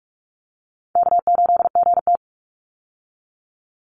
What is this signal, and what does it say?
Morse code, heard as a keyed tone, about 44 wpm, 700 Hz. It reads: K8GT
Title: Easy Call Signs Repeated 2x - 45wpm